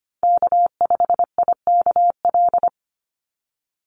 Morse code, heard as a keyed tone, about 25 wpm, 700 Hz. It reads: K5IKL
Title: USA Call Signs 25wpm